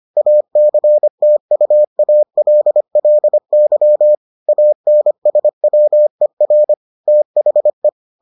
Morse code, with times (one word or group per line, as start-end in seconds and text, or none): 0.2-4.1 ACTUALLY
4.5-6.7 ANSWER
7.1-7.9 THE